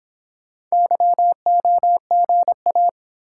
Morse code, heard as a keyed tone, about 26 wpm, 700 Hz. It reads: YOGA